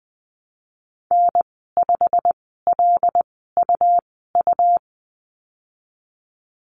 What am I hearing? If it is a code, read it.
Morse code, heard as a keyed tone, about 20 wpm, spaced as for 14 wpm, 700 Hz. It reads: N5LUU
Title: 147 Lesson 24g - Letter y - Call Signs 20wpm